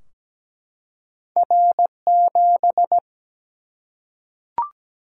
Morse code, heard as a keyed tone, about 17 wpm, 700 Hz. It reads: R7
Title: Single-Letter Single-Number 17wpm